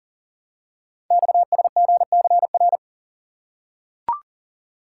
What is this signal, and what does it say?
Morse code, heard as a keyed tone, about 42 wpm, 700 Hz. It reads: XSGCR
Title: Code Groups - 5 Letters - 40wpm